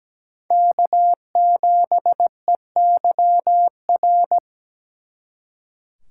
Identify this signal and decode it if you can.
Morse code, heard as a keyed tone, about 20 wpm, 700 Hz. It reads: K7EYR
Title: Speed Racing - USA Call Signs 17wpm